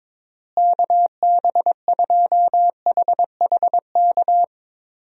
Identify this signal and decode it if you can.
Morse code, heard as a keyed tone, about 22 wpm, 700 Hz. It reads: KB2HHK
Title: USA Call Signs - 6 Characters - Repeated 2x - 22wpm